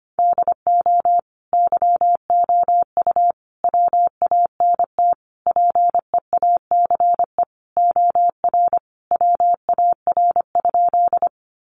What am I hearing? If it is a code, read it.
Morse code, heard as a keyed tone, about 25 wpm, 700 Hz. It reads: DO YOU WANT PEACE OR WAR?
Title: Sentences from Top 1500 Words 25wpm